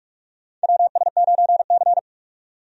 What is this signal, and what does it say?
Morse code, heard as a keyed tone, about 41 wpm, 700 Hz. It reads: WS9C
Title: USA Call Signs - 4 Characters - 45wpm